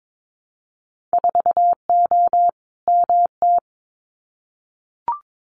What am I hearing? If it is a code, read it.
Morse code, heard as a keyed tone, about 22 wpm, 700 Hz. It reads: 4O MT
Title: Winter Field Day Exchanges Encore 22wpm